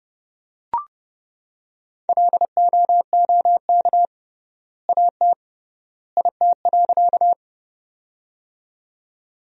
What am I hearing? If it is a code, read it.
Morse code, heard as a keyed tone, about 31 wpm, 700 Hz. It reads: LOOK AT IT.
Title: 3x Word-Spacing - Sentences from Top 100 Words - 30wpm